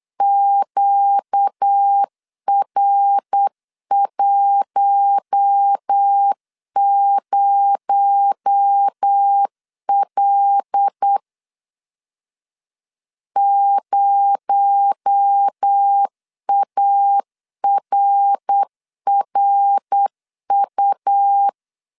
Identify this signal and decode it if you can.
Morse code, heard as a keyed tone, about 8 wpm, 800 Hz. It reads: QR10L 0ARRU
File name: CW / DET / 1 6 1500C S B 8wpm 5142